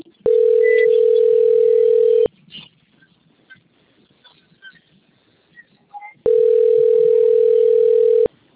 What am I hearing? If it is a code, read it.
Ringback tone (North America), 440 + 480 Hz.